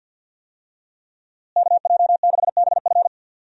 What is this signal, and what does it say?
Morse code, heard as a keyed tone, about 51 wpm, 700 Hz. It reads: KJ6BF
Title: USA Call Signs - 5 Characters - Repeated 2x - 50wpm